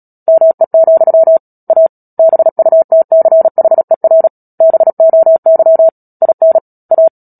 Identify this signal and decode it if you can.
Morse code, heard as a keyed tone, about 37 wpm, 650 Hz. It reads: ME, A BUTCHER BOY IN A
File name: Book Two / 10-Chapter 10 - Epilogue - 36 WPM